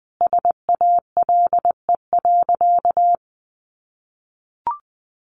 Morse code, heard as a keyed tone, about 19 wpm, 700 Hz. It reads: SALE.
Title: Sentences from Top 1400 Words 20wpm